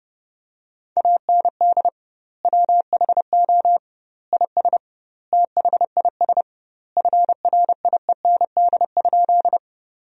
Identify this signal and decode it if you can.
Morse code, heard as a keyed tone, about 30 wpm, 700 Hz. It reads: AND WHO IS THIS FRIEND?
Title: 2x Word-Spacing - Sentences from Top 300 Words - 30wpm